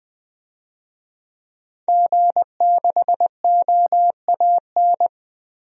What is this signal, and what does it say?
Morse code, heard as a keyed tone, about 20 wpm, 700 Hz. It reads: G6OAN
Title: DXCC-100 Call Signs - 5 Characters - 20wpm